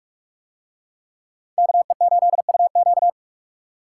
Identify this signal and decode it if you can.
Morse code, heard as a keyed tone, about 42 wpm, 700 Hz. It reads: KE8UQ